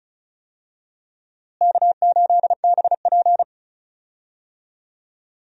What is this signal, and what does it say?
Morse code, heard as a keyed tone, about 34 wpm, 700 Hz. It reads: K8BP
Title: USA Call Signs - 4 Characters - Repeated 3x - 35wpm